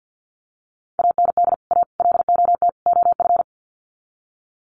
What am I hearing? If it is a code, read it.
Morse code, heard as a keyed tone, about 50 wpm, 700 Hz. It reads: AND A LOT OF
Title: Sets of 4 Words 50wpm